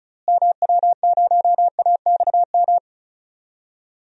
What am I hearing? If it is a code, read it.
Morse code, heard as a keyed tone, about 36 wpm, 700 Hz. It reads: MW0AXM
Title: DXCC-100 Call Signs - 6 Characters - Repeated 2x - 35wpm